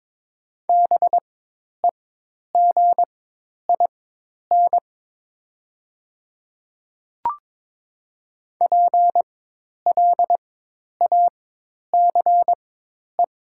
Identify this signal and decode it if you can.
Morse code, heard as a keyed tone, about 22 wpm, spaced as for 10 wpm, 700 Hz. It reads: BEGIN PLACE
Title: 4x Character Spacing - Top 500 Words - 22wpm